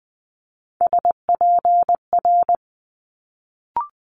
Morse code, heard as a keyed tone, about 19 wpm, 700 Hz. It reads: SPR
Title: Multi-letter Phonemes Quick Fire 20wpm